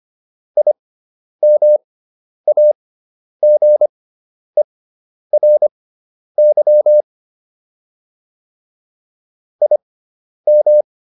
Morse code, heard as a keyed tone, about 25 wpm, spaced as for 10 wpm, 600 Hz. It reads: IMAGERY IM